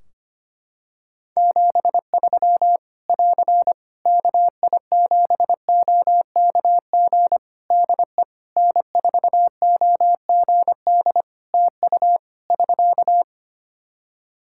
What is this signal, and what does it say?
Morse code, heard as a keyed tone, about 25 wpm, 700 Hz. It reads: 73 <AR> KI7OKG DE N4OGD TU <SK>
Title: K1USN SST QSO Practice 25wpm